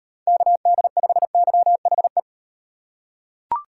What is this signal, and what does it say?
Morse code, heard as a keyed tone, about 38 wpm, 700 Hz. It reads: KD5YHE